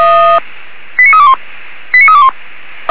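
Five-tone selective call (ZVEI), heard as quick runs of five tones, tones of about 70 ms. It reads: 89321 89321
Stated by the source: ZVEI five-tone selective call received by SDR, from the Feuerwehr Orlamünde fire brigade's audio archive.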